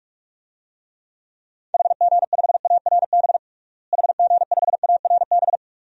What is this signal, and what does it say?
Morse code, heard as a keyed tone, about 46 wpm, 700 Hz. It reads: HG5ARB HG5ARB
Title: DXCC-100 Call Signs - 6 Characters - Repeated 2x - 45wpm